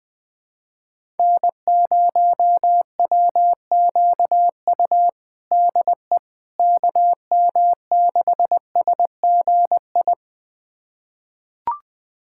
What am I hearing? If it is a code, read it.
Morse code, heard as a keyed tone, about 21 wpm, 700 Hz. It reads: N0WQU DE KM6SGI
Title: USA Callsign DE Callsign 20wpm